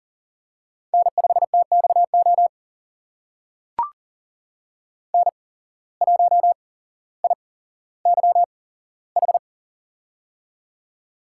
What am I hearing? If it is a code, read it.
Morse code, heard as a keyed tone, about 40 wpm, spaced as for 17 wpm, 700 Hz. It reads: N5TXO N1IYH
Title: USA Call Signs - 5 Characters - 8x Character Spacing - 40wpm